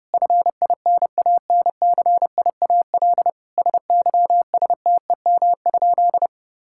Morse code, heard as a keyed tone, about 32 wpm, 700 Hz. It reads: FINANCIAL SYSTEM?
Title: Sentences from Top 1000 Words - Review Part 5 - 30wpm